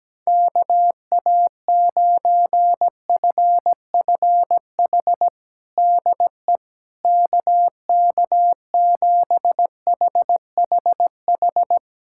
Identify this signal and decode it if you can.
Morse code, heard as a keyed tone, about 17 wpm, 700 Hz. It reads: KA9FFH DE KK7HHH
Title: USA Callsign DE Callsign - 6 Characters - 17wpm